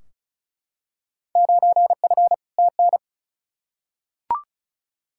Morse code, heard as a keyed tone, about 34 wpm, 700 Hz. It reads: 9F TN
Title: Field Day Exchanges Encore - 35wpm